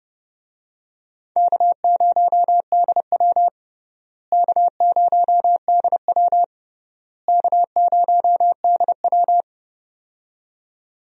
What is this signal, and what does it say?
Morse code, heard as a keyed tone, about 30 wpm, 700 Hz. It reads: K0DW K0DW K0DW
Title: USA Call Signs - 4 Characters - Repeated 3x - 30wpm